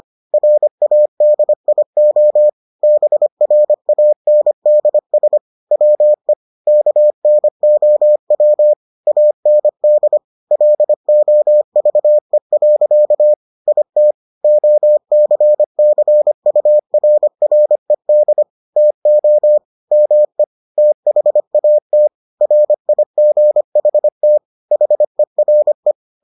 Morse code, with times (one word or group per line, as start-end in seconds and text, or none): 0.3-2.5 RADIO
2.8-5.4 BRANDS
5.7-6.3 WE
6.7-8.7 KNOW
9.1-10.2 AND
10.5-13.3 LOVE.
13.7-14.1 IT
14.4-18.4 OCCURRED
18.8-19.6 TO
19.9-20.4 ME
20.8-22.1 THAT
22.4-24.4 RIGHT
24.7-25.9 HERE